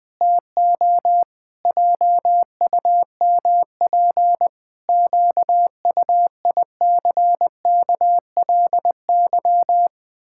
Morse code, heard as a keyed tone, about 20 wpm, 700 Hz. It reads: TO JUMP QUICKLY
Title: Pangram Challenge v2 - 20wpm